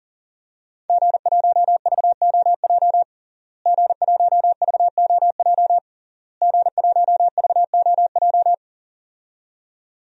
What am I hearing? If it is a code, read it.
Morse code, heard as a keyed tone, about 40 wpm, 700 Hz. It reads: G1VOJ G1VOJ G1VOJ
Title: DXCC-100 Call Signs - 5 Characters - Repeated 3x - 40wpm